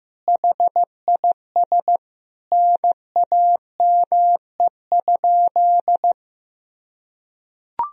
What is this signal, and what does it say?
Morse code, heard as a keyed tone, about 15 wpm, 700 Hz. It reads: HIS NAME?
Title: Sentences from Top 200 Words and Abbreviations 15wpm